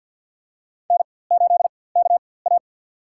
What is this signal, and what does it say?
Morse code, heard as a keyed tone, about 50 wpm, spaced as for 23 wpm, 700 Hz. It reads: N8KA
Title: USA Call Signs - 4x Character Spacing - 50wpm